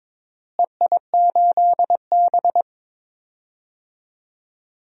Morse code, heard as a keyed tone, about 22 wpm, 700 Hz. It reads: EI8B
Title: DXCC-100 Call Signs - 4 Characters - Repeated 3x - 22wpm